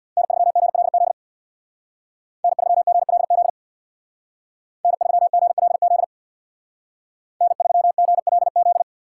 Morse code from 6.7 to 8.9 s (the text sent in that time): N3GLZ